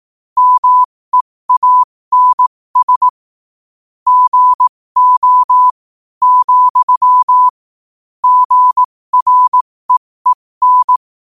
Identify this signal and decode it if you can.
Morse code, heard as a keyed tone, about 18 wpm, spaced as for 15 wpm, 1000 Hz. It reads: MEANS GO, GREEN
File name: qotdfiles / 15wpm